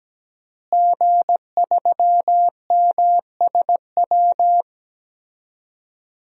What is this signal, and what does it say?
Morse code, heard as a keyed tone, about 17 wpm, 700 Hz. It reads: G3MSW
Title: DXCC-100 Call Signs - 5 Characters - Repeated 3x - 17wpm